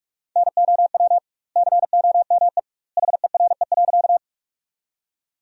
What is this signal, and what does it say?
Morse code, heard as a keyed tone, about 43 wpm, 700 Hz. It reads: NOW COME HERE.
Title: 2x Word-Spacing - Sentences from Top 200 Words - 45wpm